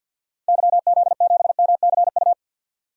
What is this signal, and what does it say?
Morse code, heard as a keyed tone, about 52 wpm, 700 Hz. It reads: YZ7MCU
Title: DXCC-100 Call Signs - 6 Characters - 50wpm